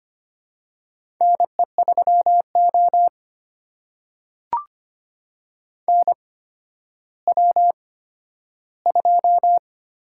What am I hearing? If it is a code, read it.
Morse code, heard as a keyed tone, about 25 wpm, spaced as for 12 wpm, 700 Hz. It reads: NE3O NW2
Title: USA Call Signs - 4 Characters - 8x Character Spacing - 25wpm